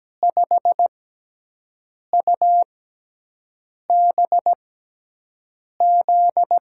Morse code, heard as a keyed tone, about 17 wpm, spaced as for 6 wpm, 700 Hz. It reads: 5UBZ
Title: USA Call Signs - 6x Character Spacing - 17wpm